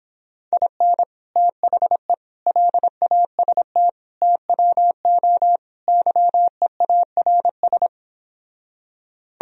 Morse code, heard as a keyed tone, about 26 wpm, 700 Hz. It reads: IN THE LAST TWO YEARS
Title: Speed-Racing - Sets of 5 Words 22wpm